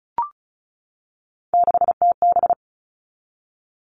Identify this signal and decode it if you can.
Morse code, heard as a keyed tone, about 34 wpm, 700 Hz. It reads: 6TB
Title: All Cut Numbers - 3-Digit 35wpm